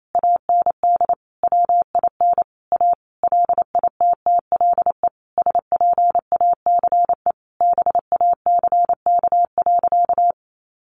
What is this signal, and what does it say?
Morse code, heard as a keyed tone, about 28 wpm, 700 Hz. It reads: AND WIN A LITTLE SPACE BACK.
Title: Sentences from Top 1500 Words 28wpm